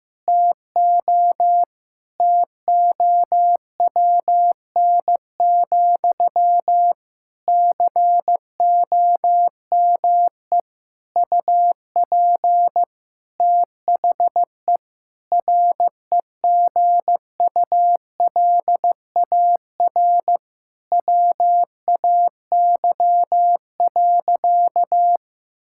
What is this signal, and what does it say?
Morse code, heard as a keyed tone, about 15 wpm, 700 Hz. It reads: TO TOWN, COME UP THE REGULAR WAY.
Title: Sentences from Top 1000 Words - Review Part 6 - 15wpm